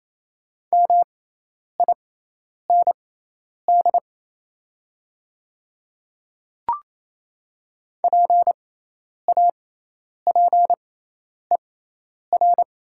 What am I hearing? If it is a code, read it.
Morse code, heard as a keyed tone, about 28 wpm, spaced as for 10 wpm, 700 Hz. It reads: MIND PAPER
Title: 6x Character Spacing - Top 500 Words - 28wpm